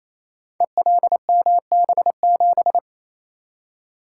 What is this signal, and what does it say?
Morse code, heard as a keyed tone, about 28 wpm, 690 Hz. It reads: ELMB7
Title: Code Groups - 5 Letter:Numbers - 28wpm